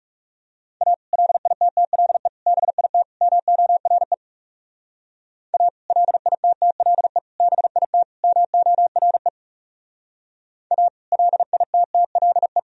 Morse code, heard as a keyed tone, about 40 wpm, 700 Hz. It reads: A LITTLE BIT MORE A LITTLE BIT MORE A LITTLE